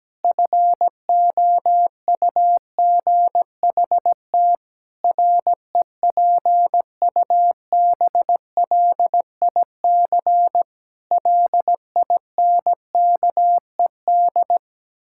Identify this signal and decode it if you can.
Morse code, heard as a keyed tone, about 17 wpm, 700 Hz. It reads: FOUGHT REPUBLIC LINKED
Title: Pangram Challenge v2 - 17wpm